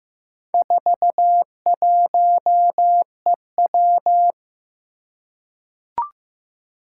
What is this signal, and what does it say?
Morse code, heard as a keyed tone, about 15 wpm, 700 Hz. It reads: 41EW
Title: Four Challenging Letter-Number Combinations 15wpm